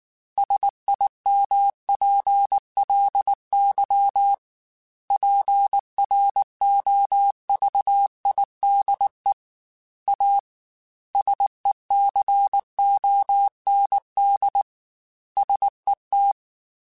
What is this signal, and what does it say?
Morse code, heard as a keyed tone, about 19 wpm, 800 Hz. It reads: SIMPLY PROVIDE A SECOND SET
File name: Technical Subjects / Technical Subjects-mp3s / Sw Pwr Sup @18